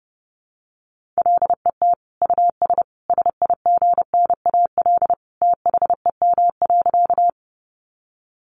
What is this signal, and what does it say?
Morse code, heard as a keyed tone, about 30 wpm, 700 Hz. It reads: LET US SIGNAL THEM.
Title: Sentences from Top 1400 Words 30wpm